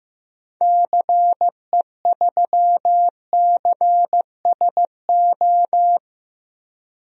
Morse code, heard as a keyed tone, about 15 wpm, 700 Hz. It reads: CE3CSO
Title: DXCC-100 Call Signs - 6 Characters - 15wpm